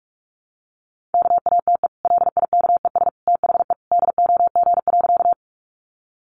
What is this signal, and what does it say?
Morse code, heard as a keyed tone, about 44 wpm, 700 Hz. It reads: KATE LIKES THE DOG.
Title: Easy Phrases 45wpm